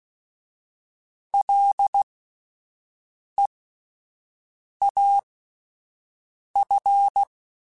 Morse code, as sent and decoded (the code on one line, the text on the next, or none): .-.. .- ..-.
LEAF